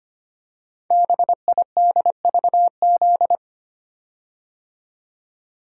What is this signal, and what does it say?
Morse code, heard as a keyed tone, about 25 wpm, 700 Hz. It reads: BIDVZ
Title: Code Groups - 5 Letter:Numbers - 25wpm